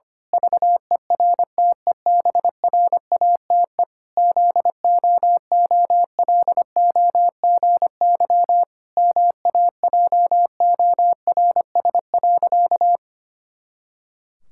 Morse code, heard as a keyed tone, about 25 wpm, 700 Hz. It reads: VERTEBRATE ZOOLOGY MAJORS.
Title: Pangram Challenge v2 - 25wpm